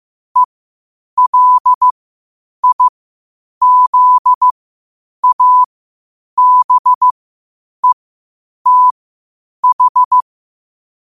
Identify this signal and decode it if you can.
Morse code, heard as a keyed tone, about 15 wpm, spaced as for 8 wpm, 1000 Hz. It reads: ELIZABETH
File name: qotdfiles / 8wpm